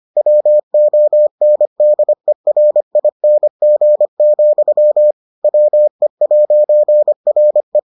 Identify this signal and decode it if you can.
Morse code, heard as a keyed tone, about 25 wpm, 600 Hz. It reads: WONDERING, WE'RE